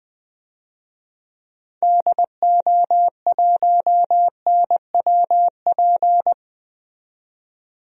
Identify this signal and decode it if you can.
Morse code, heard as a keyed tone, about 21 wpm, 700 Hz. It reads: DO1NWP
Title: DXCC-100 Call Signs - 6 Characters - 20wpm